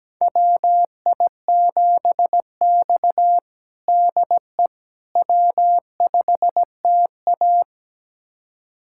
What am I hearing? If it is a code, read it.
Morse code, heard as a keyed tone, about 17 wpm, 700 Hz. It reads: WI7X DE W5TA
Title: USA Callsign DE Callsign - 4 Characters - 17wpm